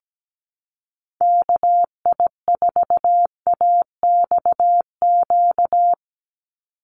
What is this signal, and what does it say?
Morse code, heard as a keyed tone, about 17 wpm, 700 Hz. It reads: KI4AXQ
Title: USA Call Signs 17wpm